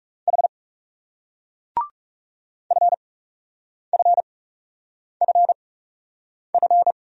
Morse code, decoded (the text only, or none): H FFFF